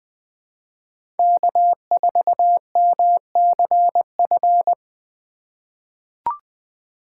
Morse code, heard as a keyed tone, about 21 wpm, 700 Hz. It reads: K4MCF
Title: USA Call Signs - 5 Characters - 3x Character Spacing - 20wpm